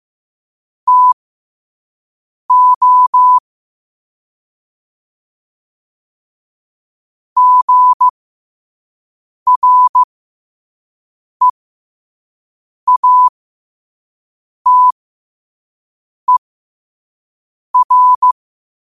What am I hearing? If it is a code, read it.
Morse code, heard as a keyed tone, about 15 wpm, spaced as for 5 wpm, 1000 Hz. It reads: TO GREATER
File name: qotdfiles / 5wpm